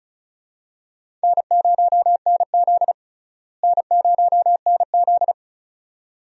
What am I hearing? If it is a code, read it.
Morse code, heard as a keyed tone, about 34 wpm, 700 Hz. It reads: N0NZ N0NZ
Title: USA Call Signs - 4 Characters - Repeated 2x - 35wpm